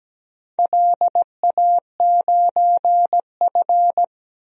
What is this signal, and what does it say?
Morse code, heard as a keyed tone, about 17 wpm, 700 Hz. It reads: LA9F
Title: DXCC-100 Call Signs - 4 Characters - 17wpm